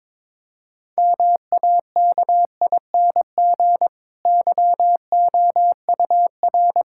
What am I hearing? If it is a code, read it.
Morse code, heard as a keyed tone, about 22 wpm, 700 Hz. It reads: MAKING YOUR
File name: Difficult Sentences with Hints - 22wpm